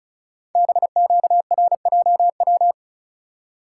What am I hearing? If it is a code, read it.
Morse code, heard as a keyed tone, about 36 wpm, 700 Hz. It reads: BQRJW
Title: Code Groups - 5 Letters - 35wpm